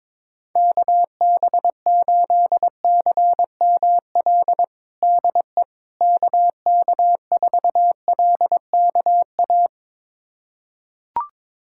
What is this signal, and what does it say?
Morse code, heard as a keyed tone, about 22 wpm, 700 Hz. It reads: KB8CML DE KK4LKA
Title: USA Callsign DE Callsign 22wpm